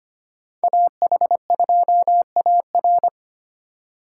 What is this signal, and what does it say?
Morse code, heard as a keyed tone, about 32 wpm, 700 Hz. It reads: AH2AR